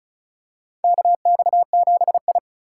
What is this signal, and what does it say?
Morse code, heard as a keyed tone, about 36 wpm, 700 Hz. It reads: KX7I
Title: USA Call Signs - 4 Characters - 4x Character Spacing - 35wpm